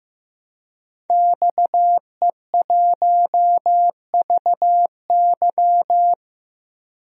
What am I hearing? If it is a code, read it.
Morse code, heard as a keyed tone, about 15 wpm, 700 Hz. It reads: XE1VY
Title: DXCC-100 Call Signs - 5 Characters - Repeated 2x - 15wpm